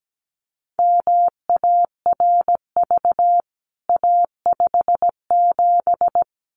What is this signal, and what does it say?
Morse code, heard as a keyed tone, about 17 wpm, 700 Hz. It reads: MARV A57
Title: CWT Exchange 17wpm